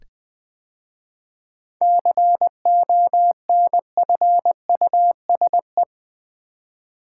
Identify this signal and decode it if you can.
Morse code, heard as a keyed tone, about 20 wpm, 700 Hz. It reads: CONFUSE
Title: 7 Letter Words - First 3 Letters Given - 20wpm